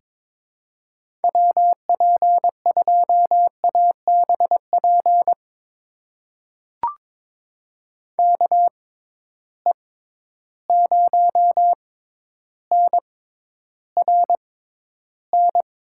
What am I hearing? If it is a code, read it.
Morse code, heard as a keyed tone, about 21 wpm, spaced as for 11 wpm, 700 Hz. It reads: WP2ABP KE0NRN
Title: USA Call Signs - 6 Characters - 6x Character Spacing - 22wpm